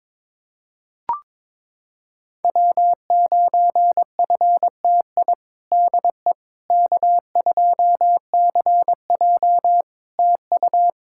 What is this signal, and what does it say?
Morse code, heard as a keyed tone, about 22 wpm, 700 Hz. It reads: W9FTI DE K2CJ TU
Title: K1USN SST QSO Practice 22wpm